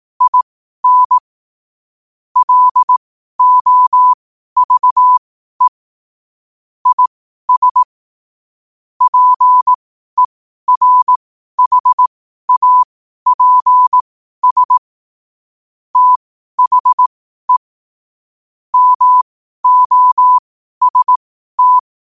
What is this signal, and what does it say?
Morse code, heard as a keyed tone, about 18 wpm, spaced as for 12 wpm, 1000 Hz. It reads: IN LOVE IS PERHAPS THE MOST
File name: qotdfiles / 12wpm